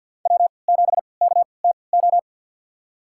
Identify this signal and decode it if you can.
Morse code, heard as a keyed tone, about 52 wpm, spaced as for 29 wpm, 700 Hz. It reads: W7KTO
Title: USA Call Signs - 5 Characters - 3x Character Spacing - 50wpm